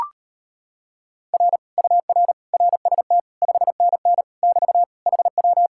Morse code, heard as a keyed tone, about 33 wpm, 700 Hz. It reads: R UR RST 5NN <BT> HW